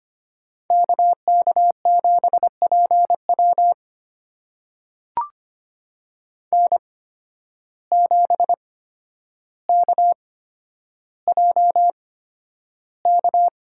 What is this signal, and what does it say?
Morse code, heard as a keyed tone, about 25 wpm, spaced as for 11 wpm, 700 Hz. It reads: KK7PW N7KJK
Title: USA Call Signs - 5 Characters - 8x Character Spacing - 25wpm